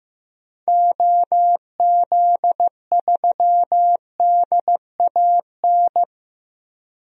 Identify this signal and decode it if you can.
Morse code, heard as a keyed tone, about 15 wpm, 700 Hz. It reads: OZ3DAN
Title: DXCC-100 Call Signs - 6 Characters - 15wpm